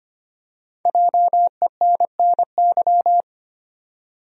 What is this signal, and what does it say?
Morse code, heard as a keyed tone, about 25 wpm, 700 Hz. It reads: JENNY
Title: USA Common Names - Max 5 Letters - 25wpm